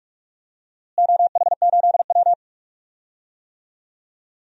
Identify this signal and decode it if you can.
Morse code, heard as a keyed tone, about 46 wpm, 700 Hz. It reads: OH8W
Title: DXCC-100 Call Signs - 4 Characters - Repeated 2x - 45wpm